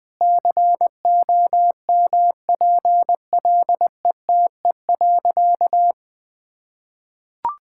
Morse code, heard as a keyed tone, about 21 wpm, 700 Hz. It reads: COMPLETE.
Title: Sentences from Top 1000 Words - Review Part 6 - 20wpm